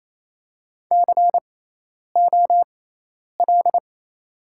Morse code, heard as a keyed tone, about 28 wpm, spaced as for 10 wpm, 690 Hz. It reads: COL
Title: 6x Character Spacing - Top 300 Words - 28wpm